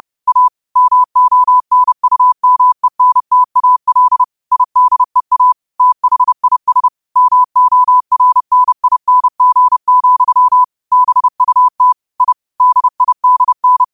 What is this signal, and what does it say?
Morse code, heard as a keyed tone, about 30 wpm, 1000 Hz. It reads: A MONUMENTAL IDEA THIS MORNING, BUT I DIDN
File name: qotdfiles / 30wpm